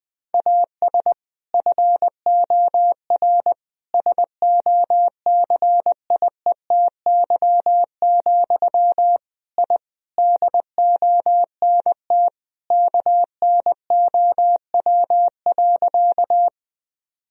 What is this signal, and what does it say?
Morse code, heard as a keyed tone, about 19 wpm, 700 Hz. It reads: AS FOR SOCIETY, I DONT KNOW.
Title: Sentences from Top 1000 Words - Review Part 4 - 20wpm